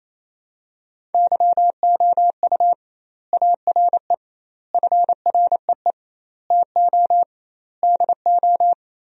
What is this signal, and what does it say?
Morse code, heard as a keyed tone, about 28 wpm, 700 Hz. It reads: YOU ARE FREE TO DO
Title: 2x Word-Spacing - Sentences from Top 500 Words - 28wpm